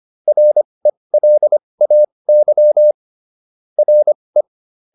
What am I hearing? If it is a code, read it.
Morse code, heard as a keyed tone, about 25 wpm, spaced as for 20 wpm, 600 Hz. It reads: RELAY RE